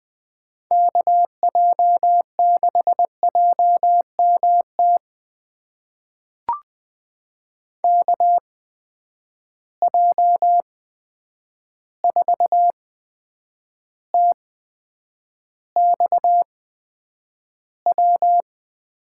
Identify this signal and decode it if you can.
Morse code, heard as a keyed tone, about 20 wpm, spaced as for 9 wpm, 700 Hz. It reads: KJ6JMT KJ4TXW
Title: USA Call Signs - 6 Characters - 8x Character Spacing - 20wpm